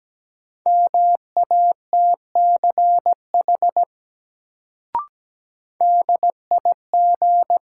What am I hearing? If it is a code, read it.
Morse code, heard as a keyed tone, about 17 wpm, 700 Hz. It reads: MATCH DIG
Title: Sentences from Top 1600 Words - New Word Drill - 17wpm